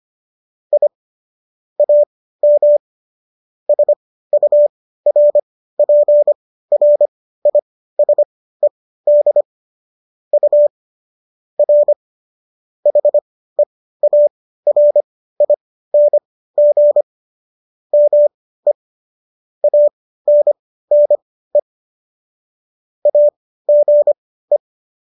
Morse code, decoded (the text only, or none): I AM SURPRISED U R HEARING ME ANNE AGE